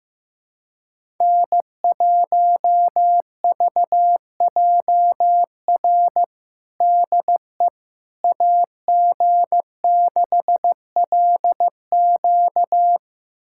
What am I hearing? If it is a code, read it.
Morse code, heard as a keyed tone, about 16 wpm, 700 Hz. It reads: N1VJR DE AG6LQ
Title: USA Callsign DE Callsign - 5 Characters - 15wpm